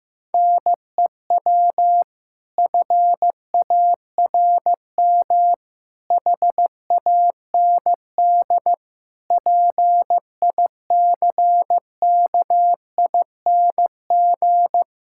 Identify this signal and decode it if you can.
Morse code, heard as a keyed tone, about 16 wpm, 700 Hz. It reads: NEW FARM HAND PICKING
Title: Pangram Challenge v2 - 15wpm